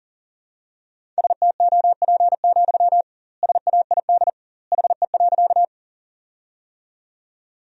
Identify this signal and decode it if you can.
Morse code, heard as a keyed tone, about 40 wpm, 700 Hz. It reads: STOP, SAID HE.
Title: 2x Word-Spacing - Sentences from Top 300 Words - 40wpm